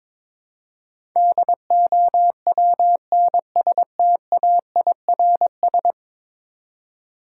Morse code, heard as a keyed tone, about 22 wpm, 700 Hz. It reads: DOWNSTAIRS
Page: morse-code-ninja-practice-22wpm